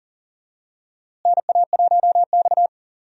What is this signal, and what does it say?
Morse code, heard as a keyed tone, about 40 wpm, 700 Hz. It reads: NA1X
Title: USA Call Signs - 4 Characters - Repeated 2x - 40wpm